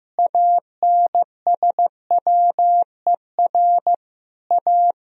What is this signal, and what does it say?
Morse code, heard as a keyed tone, about 16 wpm, 700 Hz. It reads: ANSWER A
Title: Sentences from Top 1000 Words - Review Part 1 - 15wpm